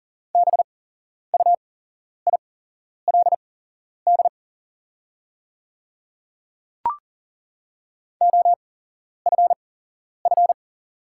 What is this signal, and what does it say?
Morse code, heard as a keyed tone, about 40 wpm, spaced as for 11 wpm, 700 Hz. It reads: BUILD OFF
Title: 8x Character Spacing - Top 200 Words - 40wpm